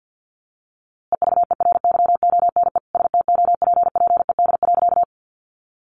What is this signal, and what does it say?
Morse code, heard as a keyed tone, about 50 wpm, 700 Hz. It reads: EVERYONE STOPPED.
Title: A Sound Of Thunder 50wpm